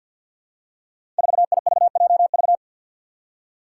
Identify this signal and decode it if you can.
Morse code, heard as a keyed tone, about 53 wpm, 700 Hz. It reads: 4IVJV
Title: Five Challenging Letter-Number Combinations 50wpm